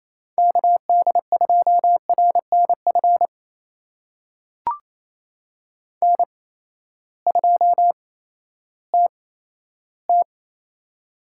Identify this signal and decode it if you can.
Morse code, heard as a keyed tone, about 28 wpm, spaced as for 14 wpm, 700 Hz. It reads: KD2RNF N2TT
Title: USA Call Signs - 8x Character Spacing - 28wpm